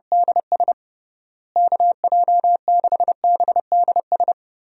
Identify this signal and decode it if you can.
Morse code, heard as a keyed tone, about 30 wpm, 700 Hz. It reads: DS KJ6BDS
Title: USA Call Signs - 6 Characters - Repeated 2x - 30wpm